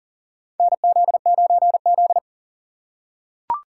Morse code, heard as a keyed tone, about 37 wpm, 700 Hz. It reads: NZ9Z
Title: USA Call Signs - 2x Character Spacing - 40wpm